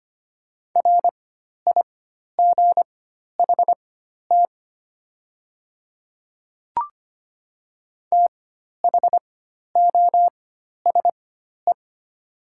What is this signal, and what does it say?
Morse code, heard as a keyed tone, about 25 wpm, spaced as for 12 wpm, 700 Hz. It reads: RIGHT THOSE
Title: 4x Character Spacing - Top 300 Words - 25wpm